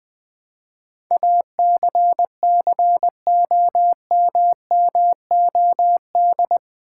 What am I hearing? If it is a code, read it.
Morse code, heard as a keyed tone, about 20 wpm, 700 Hz. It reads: ACCOMMOD
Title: Build A Word - 4 Letters At A Time - 11-Letter Words - 20wpm